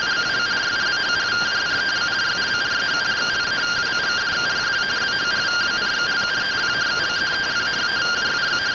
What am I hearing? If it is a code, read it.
Two-tone teleprinter signal, about 170 Hz shift.